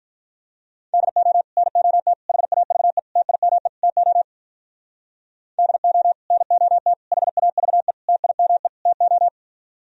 Morse code, heard as a keyed tone, about 45 wpm, 700 Hz. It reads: DO NOT HAVE TIME TO DO NOT HAVE TIME TO